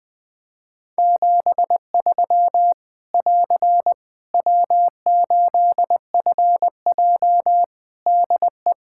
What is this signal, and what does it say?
Morse code, heard as a keyed tone, about 20 wpm, 700 Hz. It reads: 73 <AR> W8FJ DE
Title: Practice QSO - Exchange 4 - Ending - 20wpm